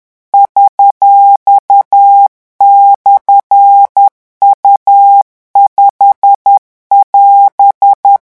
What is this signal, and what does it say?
Morse code, heard as a keyed tone, about 10 wpm, 800 Hz. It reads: $/U5<AS>
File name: morse mp3 / revision19-vitesse 10 mots